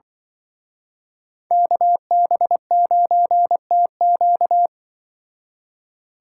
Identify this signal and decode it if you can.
Morse code, heard as a keyed tone, about 24 wpm, 700 Hz. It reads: KB9TQ
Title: Speed Racing - USA 2x2 Call Signs 20wpm